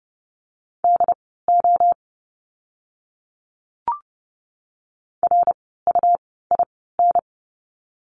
Morse code, heard as a keyed tone, about 30 wpm, spaced as for 17 wpm, 700 Hz. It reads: DO RUIN
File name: Lesson 17f - Letter u - Words 30wpm